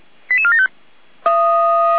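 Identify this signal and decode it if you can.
Five-tone selective call (ZVEI), heard as quick runs of five tones, tones of about 70 ms. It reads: 88475
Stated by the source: ZVEI five-tone selective call received on an SDR, recorded by feuerwehr-orlamuende (fire brigade Orlamünde).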